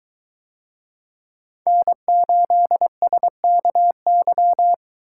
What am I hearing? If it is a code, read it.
Morse code, heard as a keyed tone, about 21 wpm, 700 Hz. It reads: N8SKY